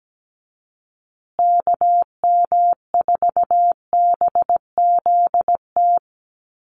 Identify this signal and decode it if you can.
Morse code, heard as a keyed tone, about 17 wpm, 700 Hz. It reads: KM4BZT
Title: USA Call Signs Repeated 2x 17wpm